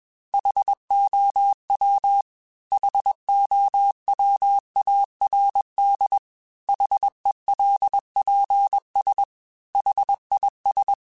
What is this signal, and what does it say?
Morse code, heard as a keyed tone, about 21 wpm, 800 Hz. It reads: HOW HOWARD HELPS HIS